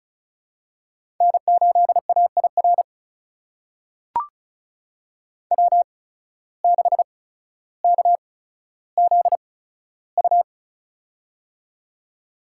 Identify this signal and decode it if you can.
Morse code, heard as a keyed tone, about 36 wpm, spaced as for 15 wpm, 700 Hz. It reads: N8AIR W6KZU
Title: USA Call Signs - 5 Characters - 8x Character Spacing - 35wpm